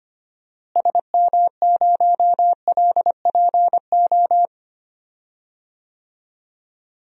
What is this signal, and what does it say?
Morse code, heard as a keyed tone, about 25 wpm, 700 Hz. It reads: SM0LPO
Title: DXCC-100 Call Signs - 6 Characters - Repeated 2x - 25wpm